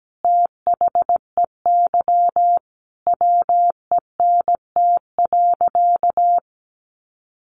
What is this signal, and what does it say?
Morse code, heard as a keyed tone, about 17 wpm, 700 Hz. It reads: THEY WENT.
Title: Sentences from Top 1300 Words 17wpm